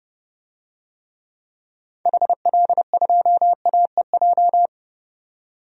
Morse code, heard as a keyed tone, about 29 wpm, 700 Hz. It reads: HL2AEJ
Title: DXCC-100 Call Signs - 6 Characters - 30wpm